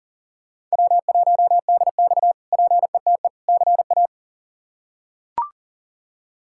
Morse code, heard as a keyed tone, about 40 wpm, 700 Hz. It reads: W1DX PETE CA
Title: K1USN SST QSO Practice - 40wpm